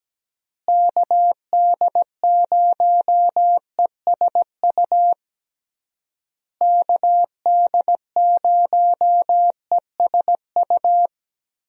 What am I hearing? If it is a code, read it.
Morse code, heard as a keyed tone, about 17 wpm, 700 Hz. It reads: KD0ESU KD0ESU
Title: USA Call Signs - 6 Characters - Repeated 3x - 17wpm